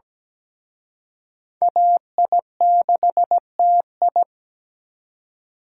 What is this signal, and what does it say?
Morse code, heard as a keyed tone, about 20 wpm, 700 Hz. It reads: AI6TI
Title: Speed Racing - USA Call Signs 17wpm